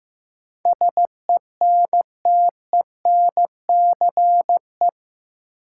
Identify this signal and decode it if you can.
Morse code, heard as a keyed tone, about 15 wpm, 700 Hz. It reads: SENTENCE
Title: Sentences from Top 1000 Words - Review Part 1 - New Word Drill - 15wpm